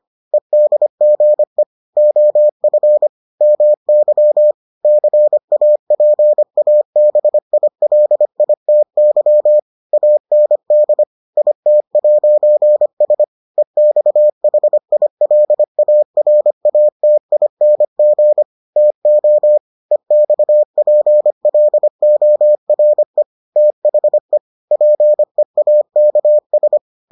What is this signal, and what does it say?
Morse code, heard as a keyed tone, about 25 wpm, 600 Hz. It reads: EDGE OF MY CAPABILITY AND IT'S EXHILARATING TO EXPLORE THE PEAKS